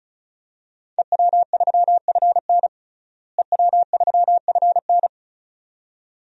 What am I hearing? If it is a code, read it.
Morse code, heard as a keyed tone, about 36 wpm, 700 Hz. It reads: EW3FN EW3FN
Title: DXCC-100 Call Signs - 5 Characters - Repeated 2x - 35wpm